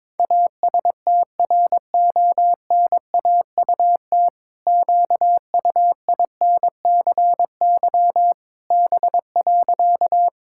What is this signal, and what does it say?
Morse code, heard as a keyed tone, about 22 wpm, 700 Hz. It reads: ASTRONAUT QUINCY B.
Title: Pangram Challenge v2 - 22wpm